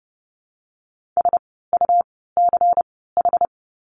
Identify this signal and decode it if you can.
Morse code, heard as a keyed tone, about 30 wpm, spaced as for 17 wpm, 700 Hz. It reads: SUCH
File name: course / 30:10wpm / 115 Lesson 19f - Letter w - Words 30wpm